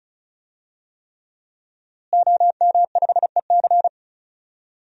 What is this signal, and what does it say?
Morse code, heard as a keyed tone, about 36 wpm, 700 Hz. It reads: OM5EC
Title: DXCC-100 Call Signs - 5 Characters - Repeated 3x - 35wpm